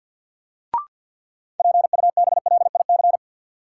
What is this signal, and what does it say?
Morse code, heard as a keyed tone, about 53 wpm, 700 Hz. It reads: PUBLIC
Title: Sets of 2 Words - New Word Drill - 50wpm